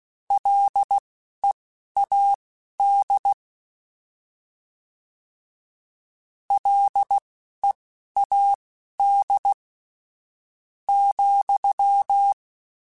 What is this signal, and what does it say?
Morse code, heard as a keyed tone, about 16 wpm, spaced as for 11 wpm, 800 Hz. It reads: LEAD LEAD ,